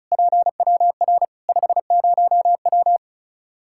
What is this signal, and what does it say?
Morse code, heard as a keyed tone, about 36 wpm, 700 Hz. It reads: PWR 50W
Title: Practice QSO - Exchange 2 - Info - 35wpm